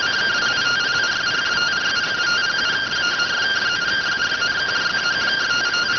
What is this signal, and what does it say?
Two-tone teleprinter signal, about 170 Hz shift.